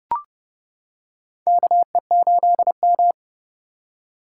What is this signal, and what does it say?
Morse code, heard as a keyed tone, about 30 wpm, 700 Hz. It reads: KE8M
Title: K1USN SST QSO Practice - 30wpm